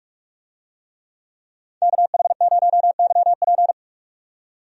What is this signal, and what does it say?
Morse code, heard as a keyed tone, about 46 wpm, 700 Hz. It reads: KH0YP
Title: USA Call Signs - 5 Characters - 45wpm